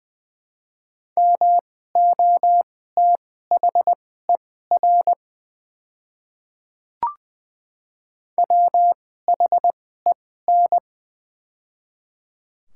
Morse code, heard as a keyed tone, about 20 wpm, spaced as for 14 wpm, 700 Hz. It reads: MOTHER WHEN
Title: 2x Character Spacing - Top 300 Words - 20wpm